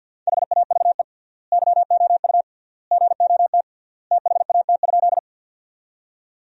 Morse code, heard as a keyed tone, about 50 wpm, 700 Hz. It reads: HAVE YOU GOT THAT?
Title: 3x Word-Spacing - Sentences from Top 300 Words - 50wpm